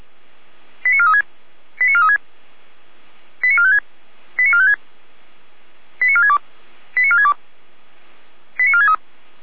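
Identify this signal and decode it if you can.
Five-tone selective call (ZVEI), heard as quick runs of five tones, tones of about 80 ms. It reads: 89436 89436 89456 89456 89462 89462 89463
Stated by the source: ZVEI five-tone call received on 455 via SDR.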